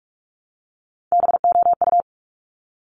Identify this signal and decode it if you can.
Morse code, heard as a keyed tone, about 46 wpm, 700 Hz. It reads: BOU
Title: Top Three Letter Combinations in Words 45wpm